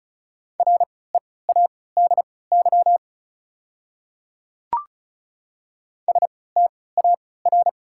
Morse code, heard as a keyed tone, about 34 wpm, spaced as for 20 wpm, 690 Hz. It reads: READY STAR
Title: 3x Character Spacing - Top 500 Words - 35wpm